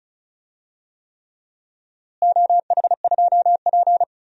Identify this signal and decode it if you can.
Morse code, heard as a keyed tone, about 34 wpm, 700 Hz. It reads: OH2P
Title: DXCC-100 Call Signs - 4 Characters - Repeated 3x - 35wpm